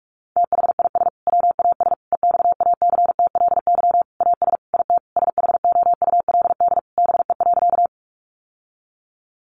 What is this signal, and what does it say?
Morse code, heard as a keyed tone, about 44 wpm, 700 Hz. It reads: THIS WAS EXACTLY AS IT SHOULD BE.
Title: Sentences from Top 900 Words 45wpm